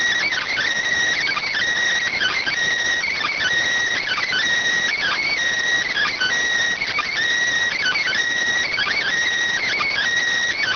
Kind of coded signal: Emergency Alert System (SAME) tones

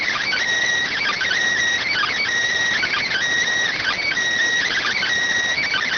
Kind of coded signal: Emergency Alert System (SAME) tones

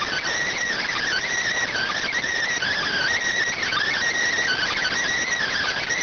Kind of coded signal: Emergency Alert System (SAME) tones

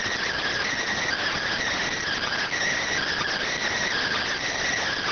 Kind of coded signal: Emergency Alert System (SAME) tones

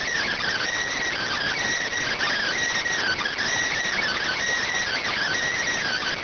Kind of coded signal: Emergency Alert System (SAME) tones

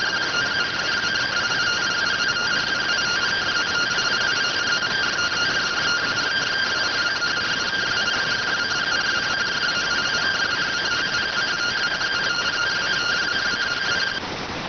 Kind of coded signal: two-tone teleprinter signal

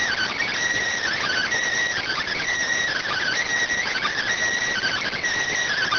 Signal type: Emergency Alert System (SAME) tones

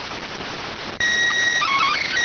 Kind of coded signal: SSTV (slow-scan television)

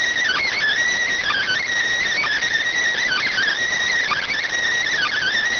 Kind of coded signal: Emergency Alert System (SAME) tones

